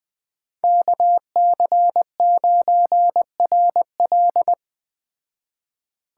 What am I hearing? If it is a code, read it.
Morse code, heard as a keyed tone, about 20 wpm, 700 Hz. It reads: KC9RL
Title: Speed Racing - USA 2x2 Call Signs 17wpm